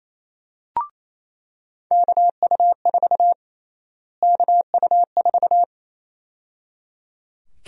Morse code, heard as a keyed tone, about 27 wpm, 700 Hz. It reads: KU4 KU4
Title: USA Call Sign Prefixes - Repeated 2x - 28wpm